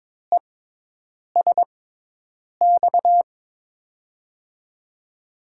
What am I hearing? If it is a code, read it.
Morse code, heard as a keyed tone, about 21 wpm, spaced as for 8 wpm, 700 Hz. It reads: ESX
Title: USA Call Signs - 6x Character Spacing - 22wpm